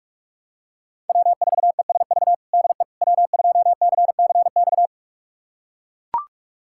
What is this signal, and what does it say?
Morse code, heard as a keyed tone, about 41 wpm, 700 Hz. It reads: W4ESV DE W2CCX
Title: USA Callsign DE Callsign - 5 Characters - 45wpm